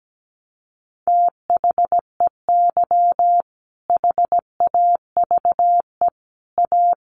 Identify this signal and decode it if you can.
Morse code, heard as a keyed tone, about 17 wpm, 700 Hz. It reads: THEY HAVE A
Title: Sentences from Top 600 Words 17wpm